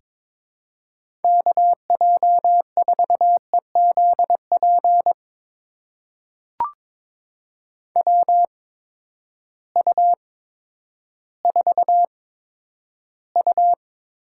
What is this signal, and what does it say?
Morse code, heard as a keyed tone, about 22 wpm, spaced as for 11 wpm, 700 Hz. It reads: KJ4EZP WU4U
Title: USA Call Signs - 8x Character Spacing - 22wpm